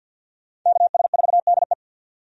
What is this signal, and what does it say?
Morse code, heard as a keyed tone, about 52 wpm, 700 Hz. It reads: KS4DE
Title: USA Call Signs - 5 Characters - 50wpm